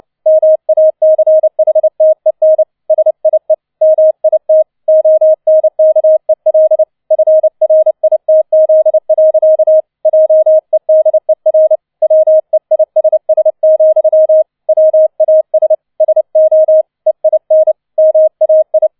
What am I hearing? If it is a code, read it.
Morse code, heard as a keyed tone, about 29 wpm, 630 Hz. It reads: MACHTEN SIE MIT ONKEL FRITZ. JEDER WEISS, WAS SO EIN MAI